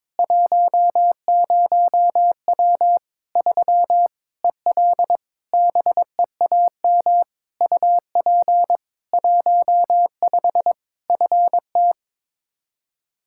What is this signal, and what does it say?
Morse code, heard as a keyed tone, about 22 wpm, 700 Hz. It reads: 10W 3 EL BEAM UP 15 FT
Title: Practice QSO - Exchange 2 - Info - 22wpm